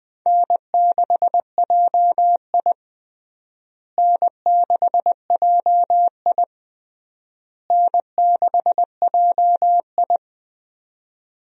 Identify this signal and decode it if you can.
Morse code, heard as a keyed tone, about 20 wpm, 700 Hz. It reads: N6JI N6JI N6JI
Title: USA Call Signs - 4 Characters - Repeated 3x - 20wpm